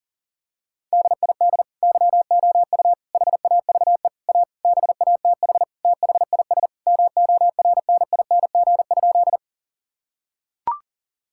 Morse code, heard as a keyed tone, about 40 wpm, 700 Hz. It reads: DID YOU HAVE A BATH THIS MORNING?
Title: Sentences from Top 1600 Words 40wpm